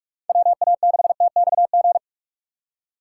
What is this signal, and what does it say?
Morse code, heard as a keyed tone, about 41 wpm, 700 Hz. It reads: WA6TXG